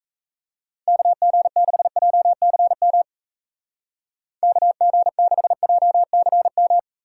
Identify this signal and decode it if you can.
Morse code, heard as a keyed tone, about 33 wpm, 700 Hz. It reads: KG6JCM KG6JCM
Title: Speed Racing - USA 2x3 Call Signs 28wpm